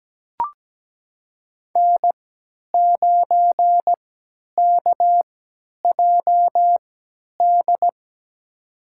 Morse code, heard as a keyed tone, about 17 wpm, spaced as for 10 wpm, 700 Hz. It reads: N9KJD
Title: USA Call Signs - 5 Characters - 3x Character Spacing - 17wpm